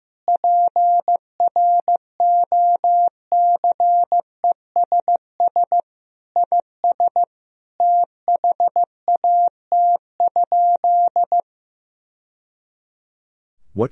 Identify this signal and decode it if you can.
Morse code, heard as a keyed tone, about 15 wpm, 700 Hz. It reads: PROCESS IS THAT?
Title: Sentences from Top 1000 Words - Review Part 4 - 15wpm